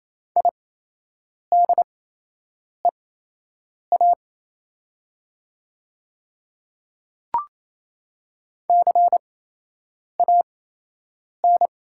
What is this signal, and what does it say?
Morse code, heard as a keyed tone, about 28 wpm, spaced as for 8 wpm, 700 Hz. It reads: IDEA CAN
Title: 8x Character Spacing - Top 300 Words - 28wpm